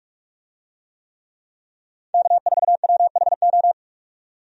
Morse code, heard as a keyed tone, about 43 wpm, 700 Hz. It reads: K4WHO
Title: USA Call Signs - 5 Characters - 45wpm